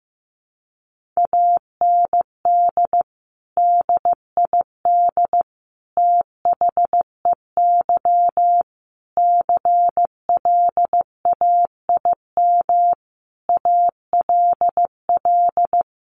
Morse code, heard as a keyed tone, about 15 wpm, 700 Hz. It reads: AND DID THEY CLAIM ALL
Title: Sentences from Top 1500 Words 15wpm